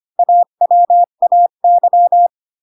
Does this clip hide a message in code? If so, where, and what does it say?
Yes: Morse code from 0.2 to 2.3 s, heard as a keyed tone, about 25 wpm, 700 Hz. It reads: AWAY